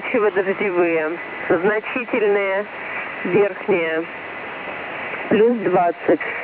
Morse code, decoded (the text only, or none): none